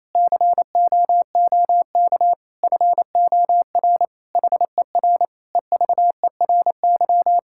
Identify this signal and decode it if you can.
Morse code, heard as a keyed tone, about 28 wpm, 700 Hz. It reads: COOK FOR HER EVERY
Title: Sentences from Top 1000 Words - Review Part 5 - 28wpm